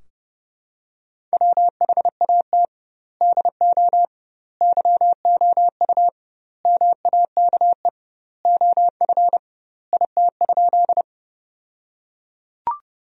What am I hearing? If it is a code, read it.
Morse code, heard as a keyed tone, about 30 wpm, 700 Hz. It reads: WHAT DO YOU MAKE OF IT?
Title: 2x Word-Spacing - Sentences from Top 100 Words - 30wpm